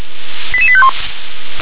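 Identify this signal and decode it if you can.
Five-tone selective call (ZVEI), heard as quick runs of five tones, tones of about 70 ms. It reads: 88851
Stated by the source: ZVEI five-tone selective call received by SDR.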